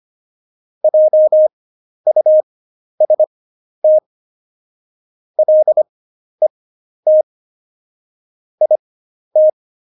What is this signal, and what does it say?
Morse code, heard as a keyed tone, about 25 wpm, spaced as for 11 wpm, 620 Hz. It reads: JUST LET IT
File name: Jokes-25-char-10-eff